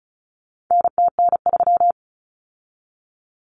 Morse code, heard as a keyed tone, about 34 wpm, 700 Hz. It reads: NTN3
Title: Common Cut Numbers - 4-Digit 35wpm